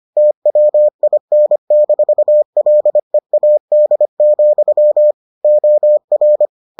Morse code, heard as a keyed tone, about 25 wpm, 600 Hz. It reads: TWIN-LEAD, OR